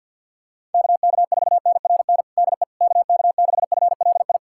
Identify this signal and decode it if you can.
Morse code, heard as a keyed tone, about 49 wpm, 700 Hz. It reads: KK4NRN DE KK6FLI
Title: USA Callsign DE Callsign - 6 Characters - 50wpm